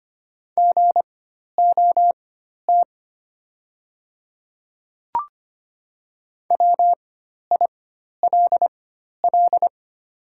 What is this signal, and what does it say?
Morse code, heard as a keyed tone, about 25 wpm, spaced as for 12 wpm, 700 Hz. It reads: GOT WILL
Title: 4x Character Spacing - Top 300 Words - 25wpm